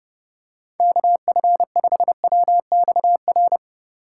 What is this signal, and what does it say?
Morse code, heard as a keyed tone, about 29 wpm, 700 Hz. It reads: KF5WXR
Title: USA Call Signs - 6 Characters - 3x Character Spacing - 30wpm